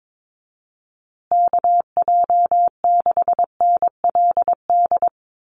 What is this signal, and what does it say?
Morse code, heard as a keyed tone, about 22 wpm, 700 Hz. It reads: KJ6NLD
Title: USA Call Signs Repeated 2x 22wpm